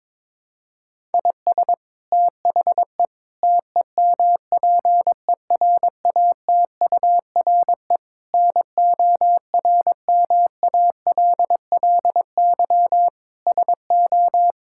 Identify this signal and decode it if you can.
Morse code, heard as a keyed tone, about 22 wpm, 700 Hz. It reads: IS THE TEMPERATURE NORMALLY SO
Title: Sentences from Top 1000 Words - Review Part 6 - 22wpm